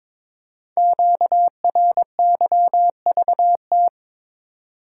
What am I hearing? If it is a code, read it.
Morse code, heard as a keyed tone, about 22 wpm, 700 Hz. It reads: QRYVT
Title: Code Groups - 5 Letter:Numbers - 22wpm